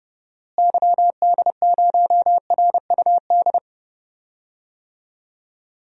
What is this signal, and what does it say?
Morse code, heard as a keyed tone, about 32 wpm, 700 Hz. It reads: YD0RUD
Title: DXCC-100 Call Signs - 6 Characters - Repeated 2x - 30wpm